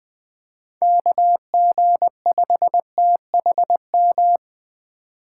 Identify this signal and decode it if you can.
Morse code, heard as a keyed tone, about 24 wpm, 700 Hz. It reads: KG5THM
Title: Speed Racing - USA 2x3 Call Signs 20wpm